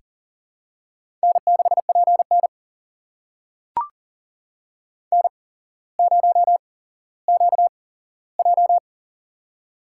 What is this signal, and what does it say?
Morse code, heard as a keyed tone, about 42 wpm, spaced as for 17 wpm, 700 Hz. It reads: N6PN N0QJ
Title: USA Call Signs - 4 Characters - 8x Character Spacing - 40wpm